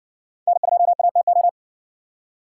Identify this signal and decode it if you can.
Morse code, heard as a keyed tone, about 58 wpm, 700 Hz. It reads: N2ATO